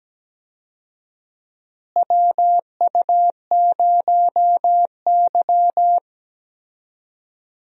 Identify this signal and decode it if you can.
Morse code, heard as a keyed tone, about 17 wpm, 700 Hz. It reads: WU0Y
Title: USA Call Signs - 4 Characters - Repeated 2x - 17wpm